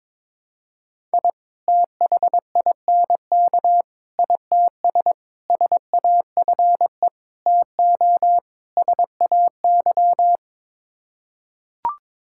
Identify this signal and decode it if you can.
Morse code, heard as a keyed tone, about 22 wpm, 700 Hz. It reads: I THINK ITS SAFE TO SAY
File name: practice / 22wpm / Sets of 6 Words - 22wpm